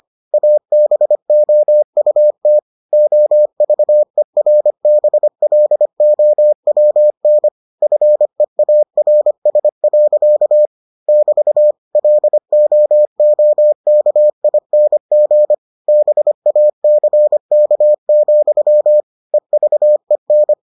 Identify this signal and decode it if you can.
Morse code, heard as a keyed tone, about 25 wpm, 600 Hz. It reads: ABOUT OVERBLOWN FEARS. <BT> LOOKING BACK, EVEN